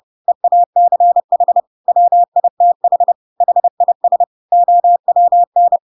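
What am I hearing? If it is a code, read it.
Morse code, heard as a keyed tone, about 30 wpm, 700 Hz. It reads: EACH WITH HIS OWN
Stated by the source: Chapter 2 of The Adventures of Pinocchio read as a book in Morse code (CW) at 30 wpm, by ik1hge.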